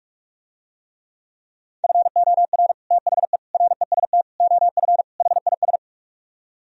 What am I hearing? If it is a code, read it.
Morse code, heard as a keyed tone, about 48 wpm, 700 Hz. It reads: FOR THE REST OF HIS